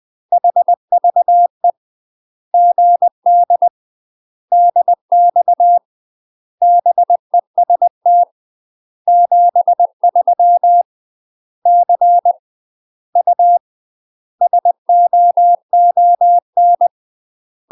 Morse code, heard as a keyed tone, about 20 wpm, 700 Hz. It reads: HVE GD DX BEST 73 C U SOON
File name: UK MASTER QSOs TRAINING / 20WPM / UK MASTER QSOs 0072